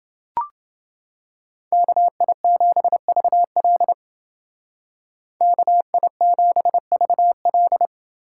Morse code, heard as a keyed tone, about 29 wpm, 700 Hz. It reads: KI7VL KI7VL